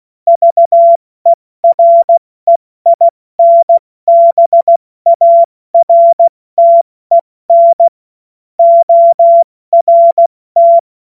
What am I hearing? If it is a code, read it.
Morse code, heard as a keyed tone, about 16 wpm, 680 Hz. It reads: VEREINBARTEN ORT
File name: Witz-23-September-2025-70-F80